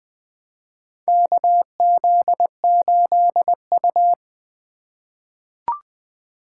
Morse code, heard as a keyed tone, about 20 wpm, 700 Hz. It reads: KZ8U
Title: USA Call Signs - 4 Characters - 2x Character Spacing - 20wpm